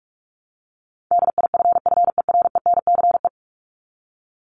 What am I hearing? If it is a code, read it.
Morse code, heard as a keyed tone, about 44 wpm, 700 Hz. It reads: DIFFERENCE
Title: ES:S Suffix - Top 100 Words with Base Words 45wpm